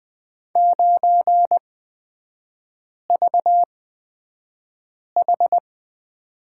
Morse code, heard as a keyed tone, about 20 wpm, spaced as for 5 wpm, 700 Hz. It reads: 9VH
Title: 250 Lesson 40e - BK - Three Characters 20wpm